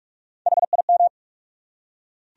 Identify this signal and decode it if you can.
Morse code, heard as a keyed tone, about 42 wpm, 700 Hz. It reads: HIM
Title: Top 100 Words 45wpm